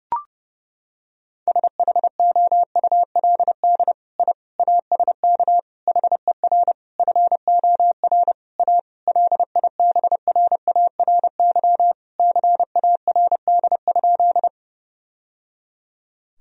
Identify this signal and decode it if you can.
Morse code, heard as a keyed tone, about 30 wpm, 700 Hz. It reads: SHOULD I ASK HER FOR A LIBRARY CARD?
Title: Sentences from Top 1000 Words - Review Part 4 - 30wpm